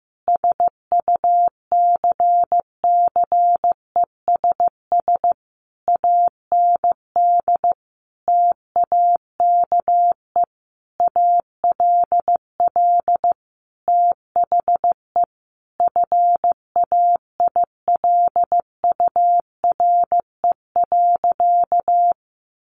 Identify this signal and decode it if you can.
Morse code, heard as a keyed tone, about 15 wpm, 700 Hz. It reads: SUCCESS AND TAKE ALL THE FAILURE.
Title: Sentences from Top 900 Words 15wpm